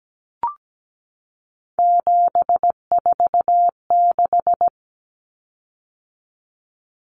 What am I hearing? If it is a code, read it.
Morse code, heard as a keyed tone, about 17 wpm, 690 Hz. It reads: 746
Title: Common Cut Numbers - 3-Digit 17wpm